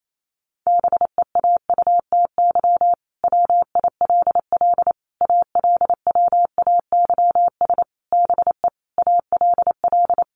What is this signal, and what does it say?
Morse code, heard as a keyed tone, about 28 wpm, 700 Hz. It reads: BEAUTY WILL ALWAYS BE ALL
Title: Sentences from Top 1300 Words 28wpm